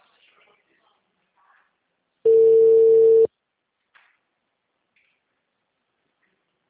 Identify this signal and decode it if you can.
Ringback tone (China, 450 Hz).
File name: out- -2007- - - .56519